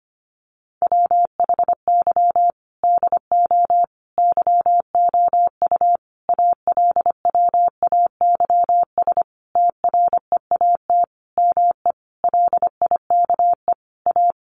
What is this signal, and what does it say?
Morse code, heard as a keyed tone, about 25 wpm, 700 Hz. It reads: WHY DO YOU ALWAYS TREAT ME LIKE A
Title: Sentences from Top 1300 Words 25wpm